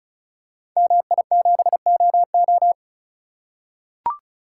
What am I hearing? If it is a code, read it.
Morse code, heard as a keyed tone, about 36 wpm, 700 Hz. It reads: MI7OO
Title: Code Groups - 5 Letter:Numbers - 35wpm